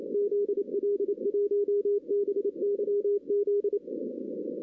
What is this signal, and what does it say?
Morse code, heard as a keyed tone, about 30 wpm, 410 Hz. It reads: ZL1BYZ
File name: 2016 arrl dx cw / 2016-ARRL-DX-CW-K5TR- - 0-9